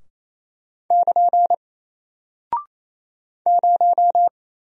Morse code, heard as a keyed tone, about 28 wpm, 700 Hz. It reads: <KN> 0
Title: Single Character (Letter:Number:Punctuation:Prosign) Alt 28wpm